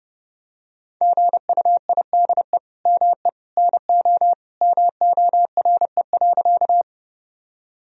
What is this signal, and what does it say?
Morse code, heard as a keyed tone, about 30 wpm, 700 Hz. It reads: GUIDE ME NO MORE.
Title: Sentences from Top 1000 Words - Review Part 3 - 30wpm